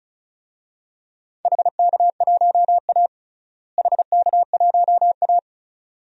Morse code, heard as a keyed tone, about 34 wpm, 690 Hz. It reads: HK1A HK1A
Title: DXCC-100 Call Signs - 4 Characters - Repeated 2x - 35wpm